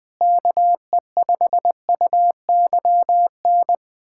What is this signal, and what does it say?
Morse code, heard as a keyed tone, about 20 wpm, 700 Hz. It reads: KE5UYN
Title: USA Callsign DE Callsign 20wpm